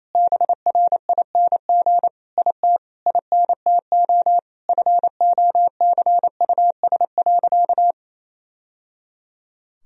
Morse code, heard as a keyed tone, about 28 wpm, 700 Hz. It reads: BRING IT INTO FOCUS.
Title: Sentences from Top 1000 Words - Review Part 4 - 28wpm